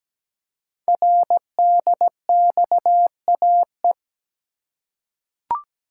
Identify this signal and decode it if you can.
Morse code, heard as a keyed tone, about 17 wpm, 700 Hz. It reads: RDXAE
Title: Code Groups - 5 Letters - 17wpm